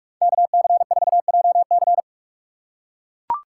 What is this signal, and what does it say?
Morse code, heard as a keyed tone, about 44 wpm, 700 Hz. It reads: KC4JC
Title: USA Call Signs - 6x Character Spacing - 45wpm